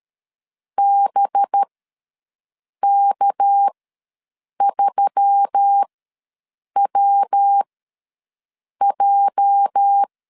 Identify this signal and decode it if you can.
Morse code, heard as a keyed tone, about 13 wpm, spaced as for 6 wpm, 800 Hz. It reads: BK3WJ
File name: LEZIONI / 10 SIGNS / 10A SLASH ws 12wpm 3443